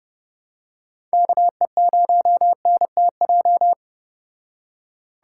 Morse code, heard as a keyed tone, about 32 wpm, 700 Hz. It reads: KE0NTJ